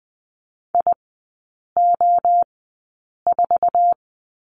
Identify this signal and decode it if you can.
Morse code, heard as a keyed tone, about 20 wpm, spaced as for 8 wpm, 700 Hz. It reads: IO4
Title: 053 Lesson 10e - Letter r - Three Characters 20wpm